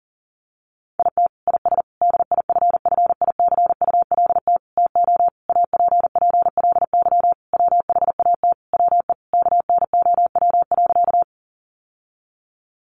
Morse code, heard as a keyed tone, about 40 wpm, 700 Hz. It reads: IT IS DIFFICULT TO APPLY WHAT WE KNOW.
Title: Sentences from Top 1200 Words 40wpm